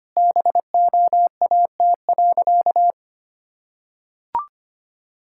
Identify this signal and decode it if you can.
Morse code, heard as a keyed tone, about 25 wpm, 700 Hz. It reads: BOAT.
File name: practice / 25wpm / Sentences with 3-Word Phrases - 25wpm